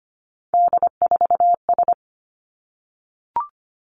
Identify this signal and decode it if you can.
Morse code, heard as a keyed tone, about 25 wpm, 700 Hz. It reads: D4S
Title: Three Challenging Letter-Number Combinations 25wpm